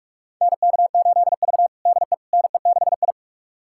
Morse code, heard as a keyed tone, about 41 wpm, 700 Hz. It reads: NK8V DE NE6I
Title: USA Callsign DE Callsign - 4 Characters - 45wpm